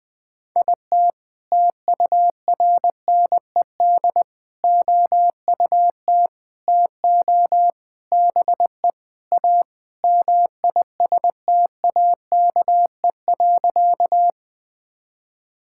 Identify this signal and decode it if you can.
Morse code, heard as a keyed tone, about 20 wpm, 700 Hz. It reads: IT TURNED OUT TO BE A MISTAKE.
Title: Sentences with 4-Word Phrases - 20wpm